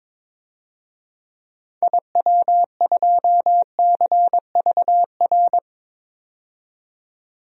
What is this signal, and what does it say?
Morse code, heard as a keyed tone, about 22 wpm, 700 Hz. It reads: IW2CVR
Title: DXCC-100 Call Signs - 6 Characters - 22wpm